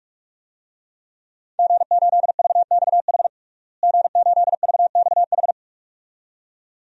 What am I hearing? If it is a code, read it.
Morse code, heard as a keyed tone, about 42 wpm, 700 Hz. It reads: G8VXH G8VXH
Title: DXCC-100 Call Signs - 5 Characters - Repeated 2x - 45wpm